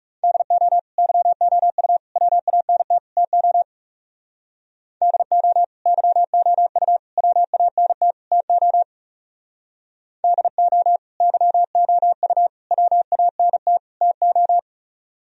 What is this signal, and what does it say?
Morse code, heard as a keyed tone, about 36 wpm, 700 Hz. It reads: DO YOU WANT TO DO YOU WANT TO DO YOU WANT TO
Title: Speed-Racing - Sets of 4 Words 30wpm